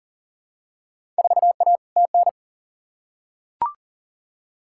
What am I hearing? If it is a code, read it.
Morse code, heard as a keyed tone, about 38 wpm, 700 Hz. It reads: 4A TN
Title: Field Day Exchanges - 40wpm